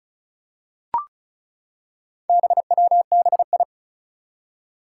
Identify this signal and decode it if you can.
Morse code, heard as a keyed tone, about 36 wpm, 700 Hz. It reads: BWBI
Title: Four Challenging Letter-Number Combinations 35wpm